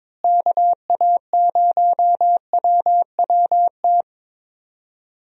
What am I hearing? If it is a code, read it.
Morse code, heard as a keyed tone, about 22 wpm, 700 Hz. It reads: KA0WWT
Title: USA Call Signs - 6 Characters - Repeated 3x - 22wpm